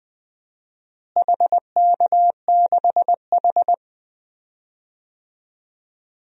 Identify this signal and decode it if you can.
Morse code, heard as a keyed tone, about 20 wpm, 700 Hz. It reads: HK6H
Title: DXCC-100 Call Signs - 4 Characters - Repeated 2x - 20wpm